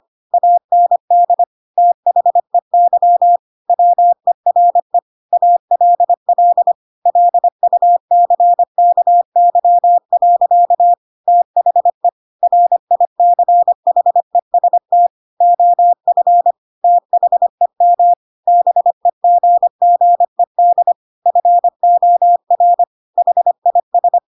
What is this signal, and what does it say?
Morse code, heard as a keyed tone, about 25 wpm, 700 Hz. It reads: AND THEY WERE ALL LUCKY. THE RICHEST OF THEM BEGGED FOR HIS
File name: Pinocchio / Pinocchio ch3 25 25 ik1hge